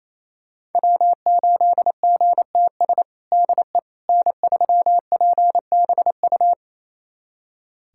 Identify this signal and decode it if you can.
Morse code, heard as a keyed tone, about 28 wpm, 700 Hz. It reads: W8GTS DE N3PBU
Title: USA Callsign DE Callsign - 5 Characters - 28wpm